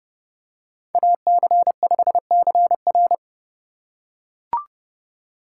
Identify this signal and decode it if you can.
Morse code, heard as a keyed tone, about 30 wpm, 700 Hz. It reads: AC5CR
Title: USA Call Signs - 5 Characters - 3x Character Spacing - 30wpm